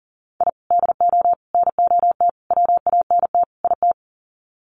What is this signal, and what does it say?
Morse code, heard as a keyed tone, about 40 wpm, 700 Hz. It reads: I DO NOT WANT IT
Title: Sets of 5 Words Encore2 40wpm